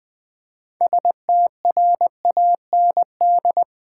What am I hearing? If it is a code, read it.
Morse code, heard as a keyed tone, about 20 wpm, 700 Hz. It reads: STRAND
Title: 6 Letter Words - First 3 Letters Given - 20wpm